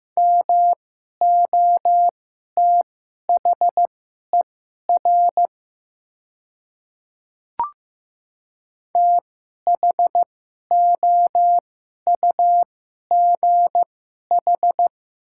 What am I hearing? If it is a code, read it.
Morse code, heard as a keyed tone, about 15 wpm, spaced as for 11 wpm, 700 Hz. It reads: MOTHER THOUGH
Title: 2x Character Spacing - Top 500 Words - 15wpm